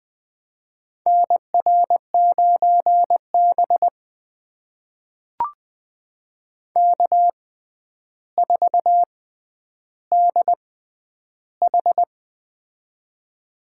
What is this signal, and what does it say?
Morse code, heard as a keyed tone, about 20 wpm, spaced as for 10 wpm, 700 Hz. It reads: NR9B K4DH
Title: USA Call Signs - 4 Characters - 6x Character Spacing - 20wpm